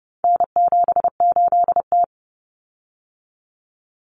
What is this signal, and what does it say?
Morse code, heard as a keyed tone, about 30 wpm, 700 Hz. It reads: N78T
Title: Common Cut Numbers - 4-Digit 30wpm